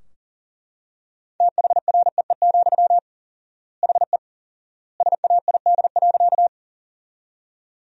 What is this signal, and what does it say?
Morse code, heard as a keyed tone, about 40 wpm, 700 Hz. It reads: THREE, HE SAID.
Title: 4x Word-Spacing - Sentences from Top 200 Words - 40wpm